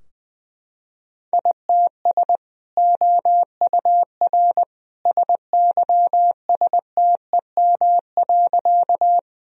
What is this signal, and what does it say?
Morse code, heard as a keyed tone, about 21 wpm, 700 Hz. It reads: ITS OUR SYSTEM.
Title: Sentences from Top 1000 Words - Review Part 3 - 20wpm